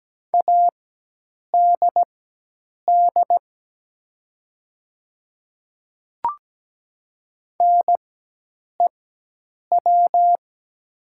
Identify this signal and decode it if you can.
Morse code, heard as a keyed tone, about 17 wpm, spaced as for 8 wpm, 700 Hz. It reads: ADD NEW
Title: 4x Character Spacing - Top 300 Words - 17wpm